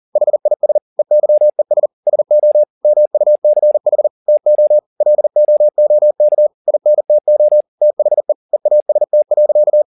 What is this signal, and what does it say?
Morse code, heard as a keyed tone, about 40 wpm, 600 Hz. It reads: HIS EYES SO MUCH TO LOOK INTO THE EAST.